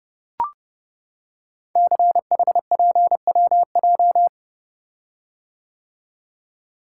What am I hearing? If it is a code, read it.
Morse code, heard as a keyed tone, about 30 wpm, 700 Hz. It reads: CHPWJ
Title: Code Groups - 5 Letters - 30wpm